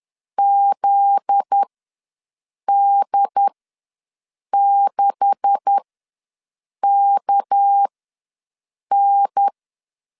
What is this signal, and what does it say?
Morse code, heard as a keyed tone, about 10 wpm, spaced as for 6 wpm, 800 Hz. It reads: ZD6KN
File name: LEZIONI / 9 286 ws 10wpm 4032